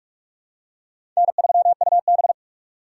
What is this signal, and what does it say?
Morse code, heard as a keyed tone, about 46 wpm, 690 Hz. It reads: N3UB
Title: USA Call Signs - 4 Characters - 3x Character Spacing - 45wpm